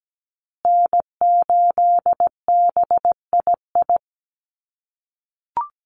Morse code, heard as a keyed tone, about 17 wpm, 700 Hz. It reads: N8BII
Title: USA Call Signs Repeated 2x 17wpm